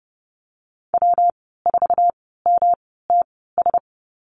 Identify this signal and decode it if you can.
Morse code, heard as a keyed tone, about 30 wpm, spaced as for 17 wpm, 700 Hz. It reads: W4MTS
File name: Lesson 19g - Letter w - Call Signs 30wpm